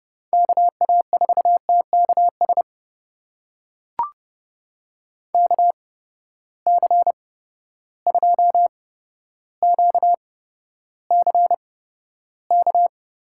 Morse code, heard as a keyed tone, about 30 wpm, spaced as for 13 wpm, 700 Hz. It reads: KA4TKS KC2QCK
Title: USA Call Signs - 6 Characters - 8x Character Spacing - 30wpm